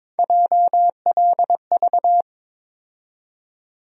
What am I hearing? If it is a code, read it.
Morse code, heard as a keyed tone, about 22 wpm, 700 Hz. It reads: JLV
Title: USA Call Sign Postfixes - 3 Letters 22wpm